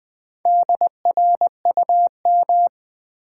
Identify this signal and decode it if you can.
Morse code, heard as a keyed tone, about 20 wpm, 700 Hz. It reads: DRUM
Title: Build A Word - 1 Letter At A Time - 4-Letter Words - 20wpm